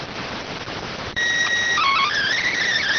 SSTV (slow-scan television), PD 290.